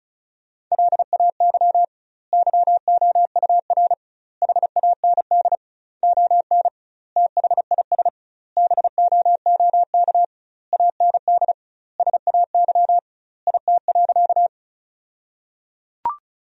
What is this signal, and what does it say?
Morse code, heard as a keyed tone, about 34 wpm, 700 Hz. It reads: LAY YOUR HAND ON THIS BOOK AND SAY IT.
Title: 2x Word-Spacing - Sentences from Top 400 Words - 35wpm